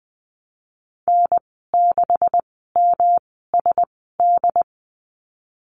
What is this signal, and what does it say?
Morse code, heard as a keyed tone, about 20 wpm, spaced as for 15 wpm, 700 Hz. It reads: N6MSD
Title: 167 Lesson 27g - Number 7 - Call Signs 20wpm